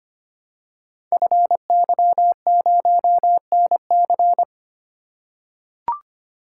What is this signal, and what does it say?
Morse code, heard as a keyed tone, about 25 wpm, 700 Hz. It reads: FY0NC
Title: Code Groups - 5 Letter:Numbers - 25wpm